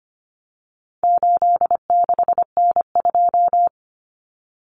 Morse code, heard as a keyed tone, about 25 wpm, 700 Hz. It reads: 86N2
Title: Common Cut Numbers - 4-Digit 25wpm